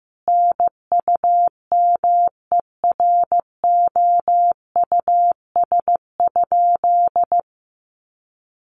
Morse code, heard as a keyed tone, about 15 wpm, 700 Hz. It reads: NUMEROUS?
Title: Sentences from Top 1400 Words 15wpm